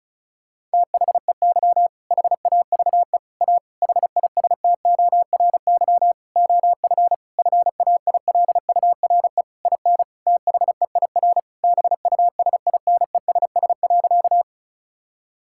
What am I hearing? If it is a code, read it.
Morse code, heard as a keyed tone, about 34 wpm, 700 Hz. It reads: THEY HAVE A HISTORY OF FAILURE IN THEIR BUSINESS.
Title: Sentences from Top 1000 Words - Review Part 6 - 35wpm